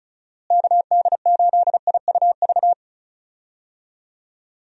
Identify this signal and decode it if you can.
Morse code, heard as a keyed tone, about 36 wpm, 700 Hz. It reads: KD8IUV
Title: USA Call Signs - 6 Characters - Repeated 2x - 35wpm